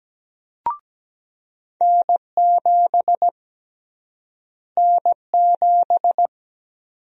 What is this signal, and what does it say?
Morse code, heard as a keyed tone, about 17 wpm, 700 Hz. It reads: N7 N7
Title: USA Call Sign Prefixes - Repeated 2x - 17wpm